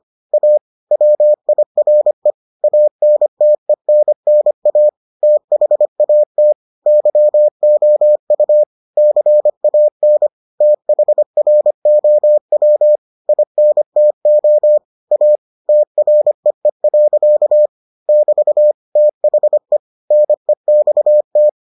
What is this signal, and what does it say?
Morse code, heard as a keyed tone, about 25 wpm, 600 Hz. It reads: A WIRE ANTENNA THAT YOU CAN THROW INTO A TREE. <BT> THE NEXT